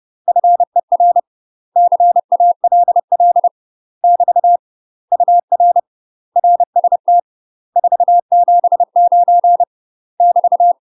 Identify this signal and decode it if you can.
Morse code, heard as a keyed tone, about 30 wpm, 700 Hz. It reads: FER CALL <BT> UR RST 479 <BT>
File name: UK MASTER QSOs TRAINING / 30WPM / UK MASTER QSOs 0048